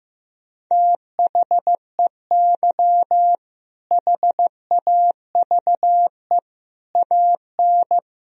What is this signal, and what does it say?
Morse code, heard as a keyed tone, about 16 wpm, 700 Hz. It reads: THEY HAVE AN
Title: Sentences from Top 1000 Words - Review Part 4 - 15wpm